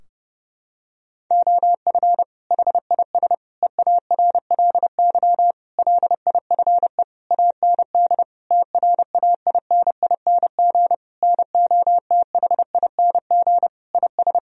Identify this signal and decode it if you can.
Morse code, heard as a keyed tone, about 30 wpm, 700 Hz. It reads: OF HIS EARLY LIFE AND TRAINING NOTHING IS
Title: Sentences from Top 1000 Words - Review Part 4 - 30wpm